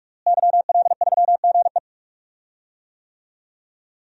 Morse code, heard as a keyed tone, about 46 wpm, 700 Hz. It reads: YL3GE
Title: DXCC-100 Call Signs - 5 Characters - Repeated 3x - 45wpm